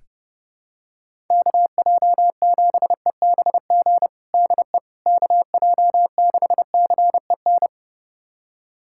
Morse code, heard as a keyed tone, about 30 wpm, 700 Hz. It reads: KJ7EBG DE KJ6CEN
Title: USA Callsign DE Callsign - 6 Characters - 30wpm